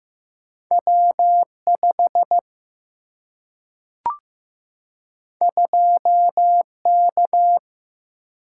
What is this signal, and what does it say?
Morse code, heard as a keyed tone, about 15 wpm, 700 Hz. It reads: W5 2K
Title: Single-Letter Single-Number 15wpm